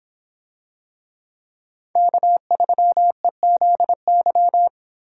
Morse code, heard as a keyed tone, about 25 wpm, 700 Hz. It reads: K3EZY